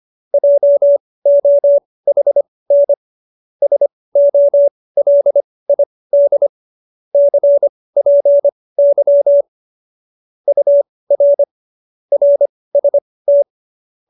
Morse code, heard as a keyed tone, about 25 wpm, spaced as for 18 wpm, 570 Hz. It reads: JOHN SOLID CPY UR RST
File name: qso216b 18